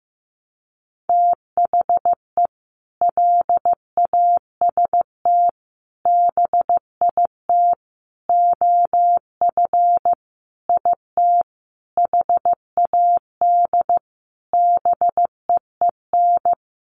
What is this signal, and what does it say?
Morse code, heard as a keyed tone, about 15 wpm, 700 Hz. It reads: THE LAST BIT OF IT HAD BEEN
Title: Sentences from Top 900 Words 15wpm